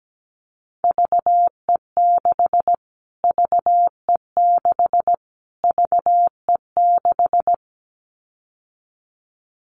Morse code, heard as a keyed tone, about 17 wpm, 700 Hz. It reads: VE6 VE6 VE6
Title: Callsign Prefixes for Top 25 DXCC - Repeated 3x - 17wpm